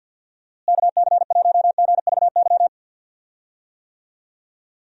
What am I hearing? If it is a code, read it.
Morse code, heard as a keyed tone, about 45 wpm, 700 Hz. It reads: KC1GVY
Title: USA Call Signs - 6 Characters - Repeated 2x - 50wpm